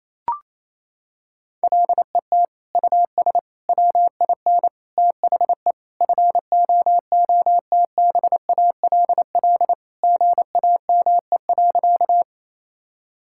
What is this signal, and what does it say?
Morse code, heard as a keyed tone, about 28 wpm, 700 Hz. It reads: LET US WIN THE FOOTBALL GAME.
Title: Sentences from Top 1600 Words 28wpm